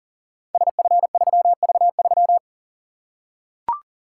Morse code, heard as a keyed tone, about 42 wpm, 700 Hz. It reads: SF3V3
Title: Code Groups - 5 Letter:Numbers - 40wpm